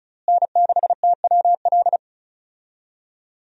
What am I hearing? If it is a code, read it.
Morse code, heard as a keyed tone, about 36 wpm, 700 Hz. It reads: N6TWL
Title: USA Call Signs - 5 Characters - Repeated 3x - 35wpm